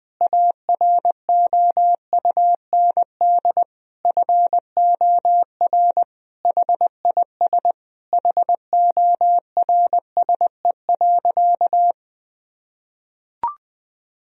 Morse code, heard as a keyed tone, about 20 wpm, 700 Hz. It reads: AROUND FOR HIS HORSE.
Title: Sentences from Top 1000 Words - Review Part 3 - 20wpm